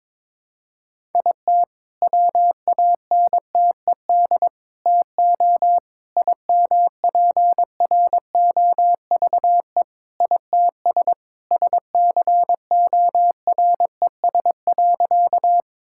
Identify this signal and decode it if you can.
Morse code, heard as a keyed tone, about 22 wpm, 700 Hz. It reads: IT WANTED TO IMPROVE ITS SCORES.
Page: morse-code-ninja-practice-22wpm